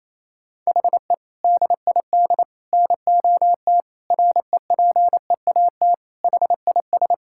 Morse code, heard as a keyed tone, about 28 wpm, 700 Hz. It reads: HE DID NOT REPEAT HIS
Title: Sentences from Top 1600 Words 28wpm